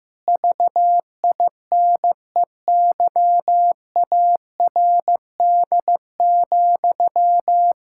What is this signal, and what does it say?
Morse code, heard as a keyed tone, about 15 wpm, 700 Hz. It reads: VINEYARD,
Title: Pangram Challenge v2 - 15wpm